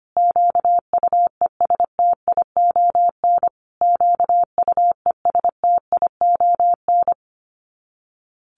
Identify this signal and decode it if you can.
Morse code, heard as a keyed tone, about 25 wpm, 700 Hz. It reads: QUESTION QUESTION
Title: Top 500 Words Repeated 25wpm